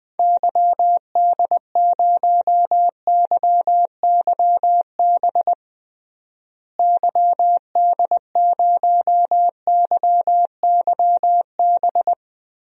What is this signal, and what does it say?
Morse code, heard as a keyed tone, about 20 wpm, 700 Hz. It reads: YD0YYB YD0YYB
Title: DXCC-100 Call Signs - 6 Characters - Repeated 3x - 20wpm